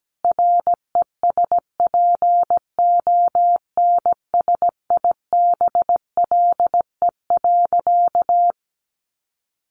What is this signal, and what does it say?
Morse code, heard as a keyed tone, about 17 wpm, 700 Hz. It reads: RESPONSIBLE.
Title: A Sound Of Thunder 17wpm